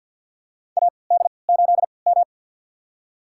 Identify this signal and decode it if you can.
Morse code, heard as a keyed tone, about 52 wpm, spaced as for 29 wpm, 700 Hz. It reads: AD8M
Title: USA Call Signs - 4 Characters - 3x Character Spacing - 50wpm